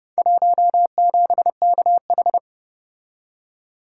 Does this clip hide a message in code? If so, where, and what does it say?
Yes: Morse code from 0.2 to 2.4 s, heard as a keyed tone, about 30 wpm, 700 Hz. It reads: 17KH